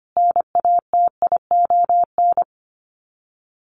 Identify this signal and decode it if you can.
Morse code, heard as a keyed tone, about 25 wpm, 700 Hz. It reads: NATION
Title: Sets of 5 Words 25wpm